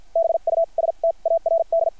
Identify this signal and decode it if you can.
Morse code, heard as a keyed tone, about 53 wpm, spaced as for 41 wpm, 660 Hz. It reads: BUSTARD
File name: archived / CRSnet